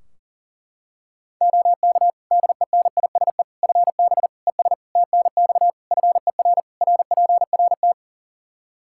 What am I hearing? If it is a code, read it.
Morse code, heard as a keyed tone, about 40 wpm, 700 Hz. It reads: OK DENISE FB ES TNX FER RPRT
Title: Practice QSO - Exchange 2 - Info - 40wpm